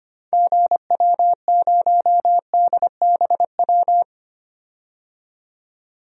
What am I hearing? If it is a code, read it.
Morse code, heard as a keyed tone, about 25 wpm, 690 Hz. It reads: GW0DBW
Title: DXCC-100 Call Signs - 6 Characters - 25wpm